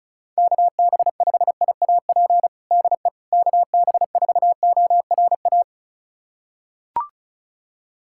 Morse code, heard as a keyed tone, about 36 wpm, 700 Hz. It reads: KB5IAP DE KB4ORA
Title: USA Callsign DE Callsign - 6 Characters - 35wpm